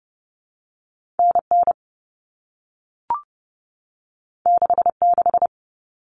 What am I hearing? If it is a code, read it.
Morse code, heard as a keyed tone, about 30 wpm, 700 Hz. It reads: N N 6 6
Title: All Cut Numbers - 2-Digit 30wpm